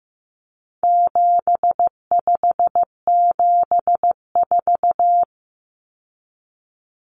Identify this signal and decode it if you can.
Morse code, heard as a keyed tone, about 15 wpm, 700 Hz. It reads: 7574
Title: Common Cut Numbers - 4-Digit 15wpm